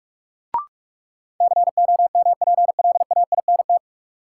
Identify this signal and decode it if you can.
Morse code, heard as a keyed tone, about 46 wpm, 700 Hz. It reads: COMPLAINT
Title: Sentences from Top 1400 Words - New Word Drill Alt - 45wpm